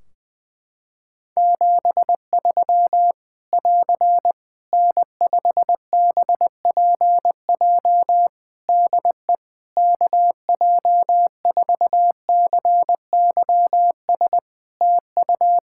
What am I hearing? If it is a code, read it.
Morse code, heard as a keyed tone, about 20 wpm, 700 Hz. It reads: 73 <AR> N5BPJ DE KJ4CYS TU
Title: Practice QSO - Exchange 4 - Ending - 20wpm